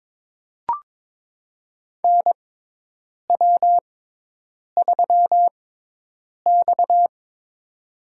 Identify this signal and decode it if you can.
Morse code, heard as a keyed tone, about 22 wpm, spaced as for 8 wpm, 700 Hz. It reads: NW3X
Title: USA Call Signs - 4 Characters - 6x Character Spacing - 22wpm